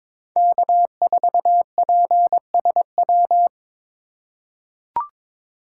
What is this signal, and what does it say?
Morse code, heard as a keyed tone, about 21 wpm, 700 Hz. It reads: K4PSW
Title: USA Call Signs - 4x Character Spacing - 22wpm